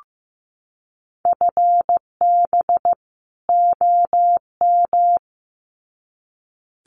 Morse code, heard as a keyed tone, about 15 wpm, 700 Hz. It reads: FB OM
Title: Common QSO Phrases 15wpm